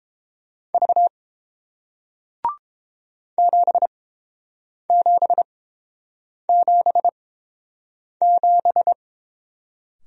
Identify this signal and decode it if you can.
Morse code, heard as a keyed tone, about 31 wpm, spaced as for 7 wpm, 700 Hz. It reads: V 7777